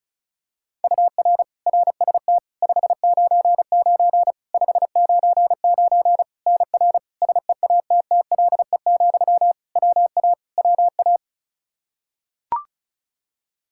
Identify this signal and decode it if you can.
Morse code, heard as a keyed tone, about 34 wpm, 700 Hz. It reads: UR RST 599 599 NR SEATTLE, WA WA
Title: Practice QSO - Exchange 1 - Beginning - 35wpm